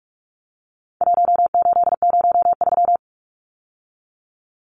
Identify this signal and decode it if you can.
Morse code, heard as a keyed tone, about 44 wpm, 700 Hz. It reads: 1803
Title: Numbers - 4-Digit 45wpm